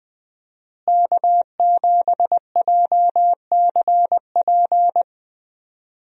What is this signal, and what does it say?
Morse code, heard as a keyed tone, about 20 wpm, 700 Hz. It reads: K7JCP
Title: USA Call Signs - 5 Characters - Repeated 3x - 20wpm